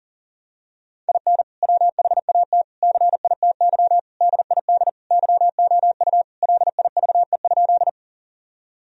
Morse code, heard as a keyed tone, about 40 wpm, 700 Hz. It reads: IN WHAT CITY DID YOU LIVE?
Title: Sentences from Top 1000 Words - Review Part 1 - 40wpm